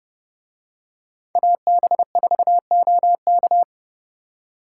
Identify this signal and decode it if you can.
Morse code, heard as a keyed tone, about 30 wpm, 700 Hz. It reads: AB4OK